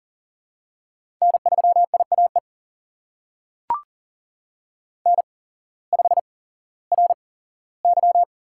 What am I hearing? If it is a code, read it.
Morse code, heard as a keyed tone, about 40 wpm, spaced as for 19 wpm, 700 Hz. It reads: N3IAE N5RY
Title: USA Call Signs - 5 Characters - 8x Character Spacing - 40wpm